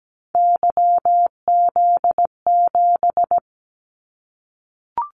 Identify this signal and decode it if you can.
Morse code, heard as a keyed tone, about 17 wpm, 700 Hz. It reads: YZ7
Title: Callsign Prefixes for Top 25 DXCC - 17wpm